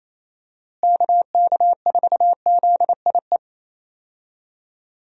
Morse code, heard as a keyed tone, about 28 wpm, 700 Hz. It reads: KK4ZIE
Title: USA Call Signs - 6 Characters - Repeated 3x - 28wpm